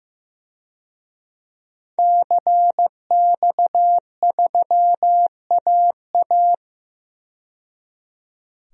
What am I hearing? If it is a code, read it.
Morse code, heard as a keyed tone, about 15 wpm, 700 Hz. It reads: CX3AA
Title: DXCC-100 Call Signs - 5 Characters - 15wpm